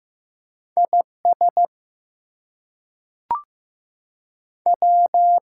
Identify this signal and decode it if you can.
Morse code, heard as a keyed tone, about 15 wpm, 700 Hz. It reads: IS W
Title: Two Letter Words - 15wpm